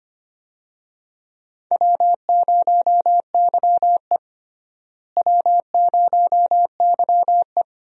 Morse code, heard as a keyed tone, about 25 wpm, 700 Hz. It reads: W0YE W0YE
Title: USA Call Signs - 4 Characters - Repeated 3x - 25wpm